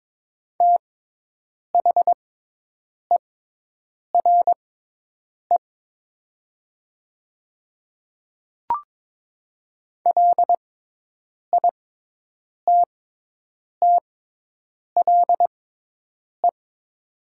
Morse code, heard as a keyed tone, about 22 wpm, spaced as for 8 wpm, 700 Hz. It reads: THERE LITTLE
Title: 6x Character Spacing - Top 500 Words - 22wpm